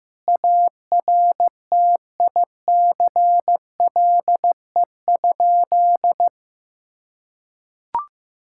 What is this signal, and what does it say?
Morse code, heard as a keyed tone, about 15 wpm, 700 Hz. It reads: ARTICLE?
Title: Sentences from Top 1000 Words - Review Part 5 - 15wpm